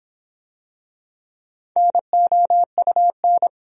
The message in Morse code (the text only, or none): NOUN